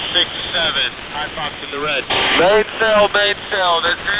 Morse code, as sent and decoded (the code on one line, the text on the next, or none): none